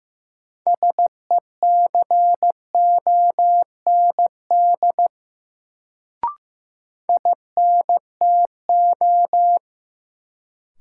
Morse code, heard as a keyed tone, about 15 wpm, 700 Hz. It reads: SECOND INTO
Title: Sets of 2 Words - New Word Drill - 15wpm